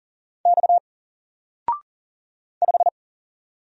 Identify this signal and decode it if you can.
Morse code, heard as a keyed tone, about 40 wpm, 700 Hz. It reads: X 5
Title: Single Letter-Number Alt 40wpm